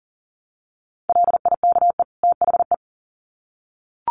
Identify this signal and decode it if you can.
Morse code, heard as a keyed tone, about 40 wpm, 700 Hz. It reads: LIKE THE
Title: Sets of 2 Words 40wpm